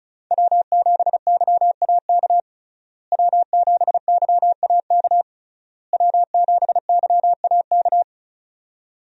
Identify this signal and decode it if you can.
Morse code, heard as a keyed tone, about 34 wpm, 700 Hz. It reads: W7YAK W7YAK W7YAK
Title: USA Call Signs - 5 Characters - Repeated 3x - 35wpm